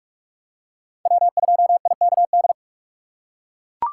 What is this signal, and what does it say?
Morse code, heard as a keyed tone, about 42 wpm, 700 Hz. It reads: W2IKD
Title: USA Call Signs - 5 Characters - 2x Character Spacing - 45wpm